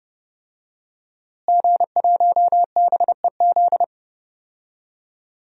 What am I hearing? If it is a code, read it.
Morse code, heard as a keyed tone, about 32 wpm, 700 Hz. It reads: G1BEZ
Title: DXCC-100 Call Signs - 5 Characters - Repeated 2x - 30wpm